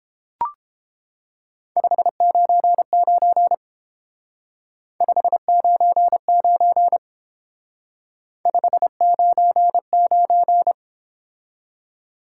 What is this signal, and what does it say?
Morse code, heard as a keyed tone, about 29 wpm, 700 Hz. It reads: 599 599 599